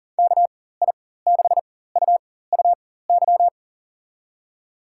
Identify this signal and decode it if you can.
Morse code, heard as a keyed tone, about 38 wpm, spaced as for 18 wpm, 700 Hz. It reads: KI6UUY
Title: USA Call Signs - 6 Characters - 4x Character Spacing - 40wpm